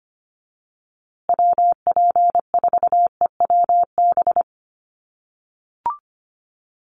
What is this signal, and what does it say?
Morse code, heard as a keyed tone, about 25 wpm, 700 Hz. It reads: WP4EWB
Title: USA Call Signs Repeated 2x 25wpm